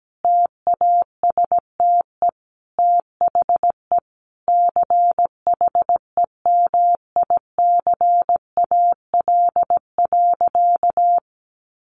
Morse code, heard as a keyed tone, about 17 wpm, 700 Hz. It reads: TASTE THE CHEMICAL.
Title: Sentences from Top 1200 Words 17wpm